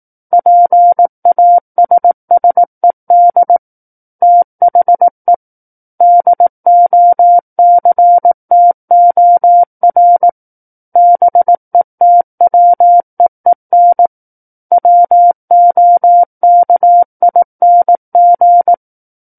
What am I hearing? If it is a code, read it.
Morse code, heard as a keyed tone, about 18 wpm, 700 Hz. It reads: PASSED THE DOCTOR BETWEEN WOKING
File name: Book One / 09-Chapter 9 - 18 WPM